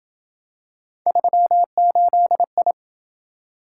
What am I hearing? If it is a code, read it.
Morse code, heard as a keyed tone, about 26 wpm, 700 Hz. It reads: 38I